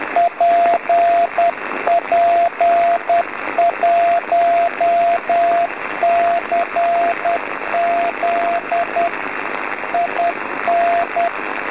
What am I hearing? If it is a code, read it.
Morse code, heard as a keyed tone, about 10 wpm, 690 Hz, with radio noise. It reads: PP1CZ IN
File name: PP1CZ B